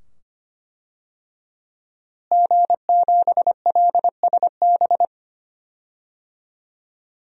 Morse code, heard as a keyed tone, about 25 wpm, 700 Hz. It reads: G7LSB
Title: DXCC-100 Call Signs - 5 Characters - Repeated 2x - 25wpm